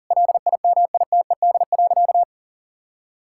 Morse code, heard as a keyed tone, about 37 wpm, 700 Hz. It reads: LIMITED.
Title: Sentences from Top 1000 Words - Review Part 6 - 40wpm